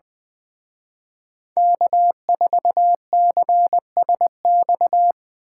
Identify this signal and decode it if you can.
Morse code, heard as a keyed tone, about 23 wpm, 700 Hz. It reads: K4CSX
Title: Speed Racing - USA 1x3 Call Signs 20wpm